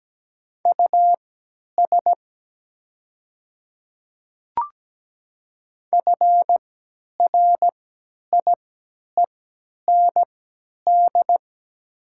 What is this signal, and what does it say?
Morse code, heard as a keyed tone, about 17 wpm, spaced as for 10 wpm, 690 Hz. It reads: US FRIEND
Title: 3x Character Spacing - Top 300 Words - 17wpm